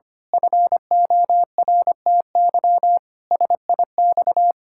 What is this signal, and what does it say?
Morse code, heard as a keyed tone, about 25 wpm, 700 Hz. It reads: FORTY SIX
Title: Pangram Challenge v2 - 25wpm